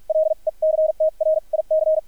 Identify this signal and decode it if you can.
Morse code, heard as a keyed tone, about 50 wpm, spaced as for 40 wpm, 630 Hz. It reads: LECTRIC
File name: archived / CRSnet